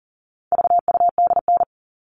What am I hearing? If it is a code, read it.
Morse code, heard as a keyed tone, about 40 wpm, 700 Hz. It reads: VUDN
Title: All Cut Numbers - 4-Digit 40wpm